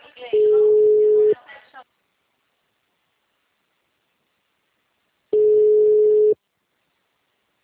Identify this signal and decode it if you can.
Ringback tone (Europe, 425 Hz), with speech.